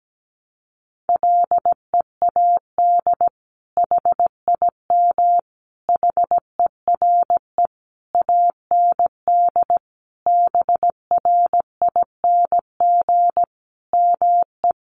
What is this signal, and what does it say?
Morse code, heard as a keyed tone, about 17 wpm, 700 Hz. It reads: LEAD HIM HERE AND BRING ME
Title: Sentences from Top 1200 Words 17wpm